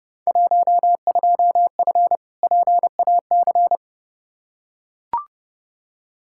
Morse code, heard as a keyed tone, about 30 wpm, 700 Hz. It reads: 12F PAC
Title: Field Day Exchanges Encore - 30wpm